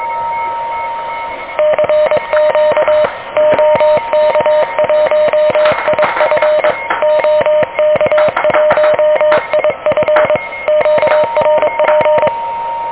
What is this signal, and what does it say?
Morse code, heard as a keyed tone, about 22 wpm, 600 Hz, with noise. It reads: CQ OK1IF OK1IF QRP
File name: OK QRP 2023 / OK QRP 2023